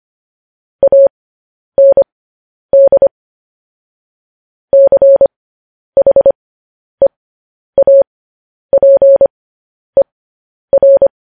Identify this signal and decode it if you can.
Morse code, heard as a keyed tone, about 25 wpm, spaced as for 10 wpm, 550 Hz. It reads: AND CHEAPER
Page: News Headlines in Morse Code at 10 WPM